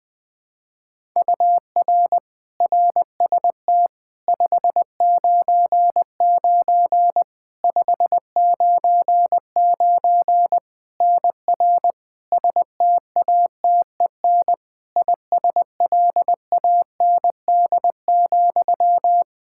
Morse code, heard as a keyed tone, about 20 wpm, 700 Hz. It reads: UR RST 599 599 NR STATEN ISLAND,
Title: Practice QSO - Exchange 1 - Beginning - 20wpm